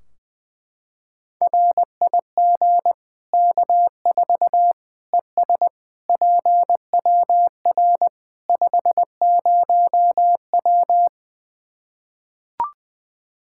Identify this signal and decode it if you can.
Morse code, heard as a keyed tone, about 21 wpm, 700 Hz. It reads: RIG K4 ES PWR 50W
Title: Practice QSO - Exchange 2 - Info - 20wpm